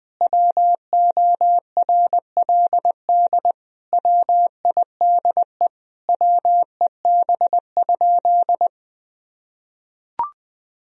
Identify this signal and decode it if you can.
Morse code, heard as a keyed tone, about 20 wpm, 700 Hz. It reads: WORLD WIDE WEB?
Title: Sentences from Top 1000 Words - Review Part 4 - 20wpm